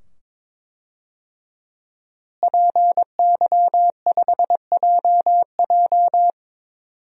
Morse code, heard as a keyed tone, about 22 wpm, 700 Hz. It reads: PY5JJ
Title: DXCC-100 Call Signs - 5 Characters - Repeated 2x - 22wpm